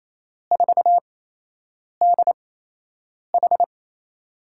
Morse code, heard as a keyed tone, about 29 wpm, spaced as for 8 wpm, 700 Hz. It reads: 4DH
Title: USA Call Signs - 4 Characters - 8x Character Spacing - 28wpm